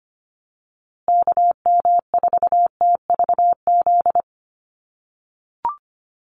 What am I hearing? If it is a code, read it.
Morse code, heard as a keyed tone, about 25 wpm, 700 Hz. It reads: KM4TVZ
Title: USA Call Signs Repeated 3x 25wpm